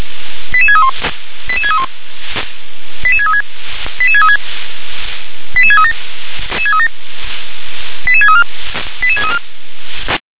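Five-tone selective call (ZVEI), heard as quick runs of five tones, tones of about 70 ms. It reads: 88631 88631 88636 88636 88637 88634 88634
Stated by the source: ZVEI five-tone call received on SDR.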